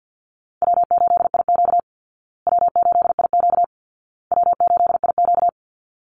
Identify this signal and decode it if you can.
Morse code, heard as a keyed tone, about 50 wpm, 700 Hz. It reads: W8IQ W8IQ W8IQ
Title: Call Signs Repeated 2x - 50wpm